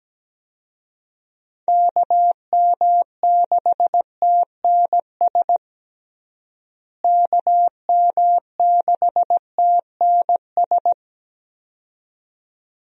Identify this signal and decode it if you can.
Morse code, heard as a keyed tone, about 17 wpm, 700 Hz. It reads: KM6TNS KM6TNS
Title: USA Call Signs - 6 Characters - Repeated 2x - 17wpm